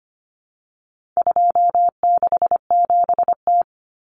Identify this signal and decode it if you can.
Morse code, heard as a keyed tone, about 25 wpm, 700 Hz. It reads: 267T
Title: Common Cut Numbers - 4-Digit 25wpm